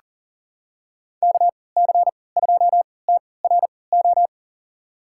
Morse code, heard as a keyed tone, about 37 wpm, spaced as for 22 wpm, 700 Hz. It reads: KC2TRO
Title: USA Call Signs - 6 Characters - 3x Character Spacing - 40wpm